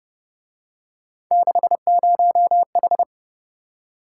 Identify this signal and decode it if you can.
Morse code, heard as a keyed tone, about 31 wpm, 700 Hz. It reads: 60H
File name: Speed-Racing - Three Letter-Number Combinations - 25wpm